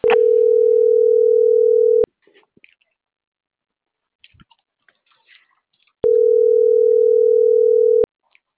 Ringback tone (North America), 440 + 480 Hz.